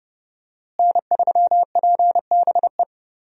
Morse code, heard as a keyed tone, about 30 wpm, 700 Hz. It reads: N3PBE
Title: USA Call Signs - 5 Characters - Repeated 3x - 30wpm